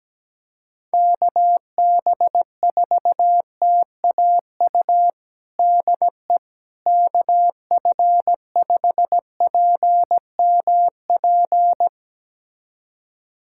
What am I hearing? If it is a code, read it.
Morse code, heard as a keyed tone, about 17 wpm, 700 Hz. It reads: KB4TAU DE KF5PMP
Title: USA Callsign DE Callsign - 6 Characters - 17wpm